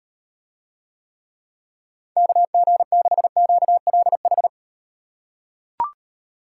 Morse code, heard as a keyed tone, about 38 wpm, 700 Hz. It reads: KG6QLH